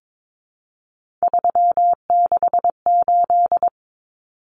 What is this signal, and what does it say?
Morse code, heard as a keyed tone, about 22 wpm, 700 Hz. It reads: 368
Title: Numbers - 3-Digit 22wpm